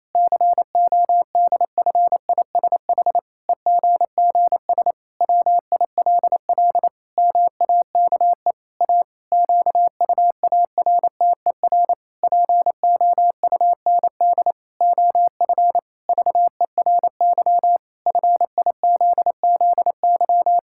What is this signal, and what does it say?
Morse code, heard as a keyed tone, about 28 wpm, 700 Hz. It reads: CODFISH EGGS WILL MAKE A QUARTER POUND OF VERY FIZZY
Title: Pangram Challenge v2 - 28wpm